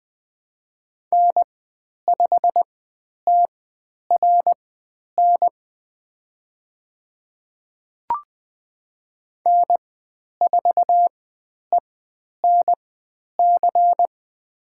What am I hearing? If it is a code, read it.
Morse code, heard as a keyed tone, about 20 wpm, spaced as for 10 wpm, 700 Hz. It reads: N5TRN N4ENC
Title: 102 Lesson 17g - Letter u - Call Signs 20wpm